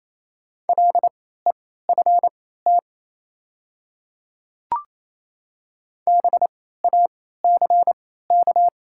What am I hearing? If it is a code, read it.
Morse code, heard as a keyed tone, about 28 wpm, spaced as for 16 wpm, 690 Hz. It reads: LEFT BACK
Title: 3x Character Spacing - Top 500 Words - 28wpm